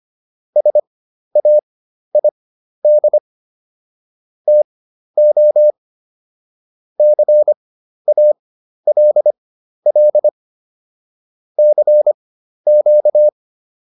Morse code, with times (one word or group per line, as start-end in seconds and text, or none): 0.6-3.2 SAID
4.5-5.7 TO
7.0-10.3 CALL
11.6-13.3 CQ